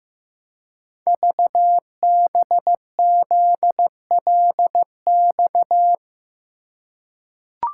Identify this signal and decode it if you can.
Morse code, heard as a keyed tone, about 15 wpm, 700 Hz. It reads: VBZLX
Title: Code Groups - 5 Letters - 15wpm